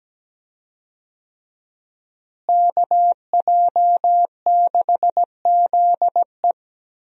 Morse code, heard as a keyed tone, about 17 wpm, 700 Hz. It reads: KJ6ZE
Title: USA Call Signs - 5 Characters - Repeated 3x - 17wpm